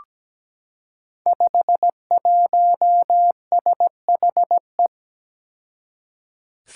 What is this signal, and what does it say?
Morse code, heard as a keyed tone, about 17 wpm, 700 Hz. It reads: 51SHE
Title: Five Challenging Letter-Number Combinations 17wpm